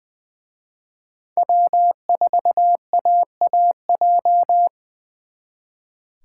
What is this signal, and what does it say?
Morse code, heard as a keyed tone, about 24 wpm, 700 Hz. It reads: W4AAJ
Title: Speed Racing - USA 1x3 Call Signs 20wpm